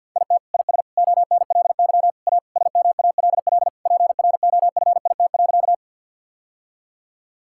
Morse code, heard as a keyed tone, about 50 wpm, 700 Hz. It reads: IT IS ONLY A SMALL PROFIT.
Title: Sentences from Top 1000 Words - Review Part 5 - 50wpm